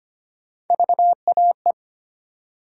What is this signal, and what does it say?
Morse code, heard as a keyed tone, about 25 wpm, 700 Hz. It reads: VAE
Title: USA Call Sign Postfixes - 3 Letters 25wpm